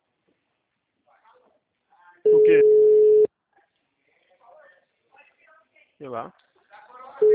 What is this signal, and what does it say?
Ringback tone (Europe, 425 Hz), with speech.